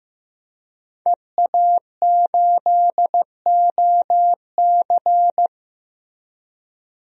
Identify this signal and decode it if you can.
Morse code, heard as a keyed tone, about 15 wpm, 700 Hz. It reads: EA8OC
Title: DXCC-100 Call Signs - 5 Characters - Repeated 3x - 15wpm